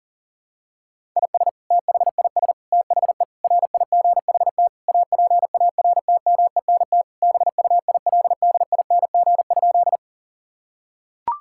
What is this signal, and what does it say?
Morse code, heard as a keyed tone, about 37 wpm, 700 Hz. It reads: IS THIS THE RIGHT APARTMENT BUILDING?
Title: Sentences from Top 1000 Words - Review Part 6 - 40wpm